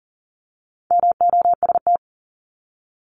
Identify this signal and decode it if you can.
Morse code, heard as a keyed tone, about 38 wpm, 700 Hz. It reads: MOST
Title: Top 200 Words 40wpm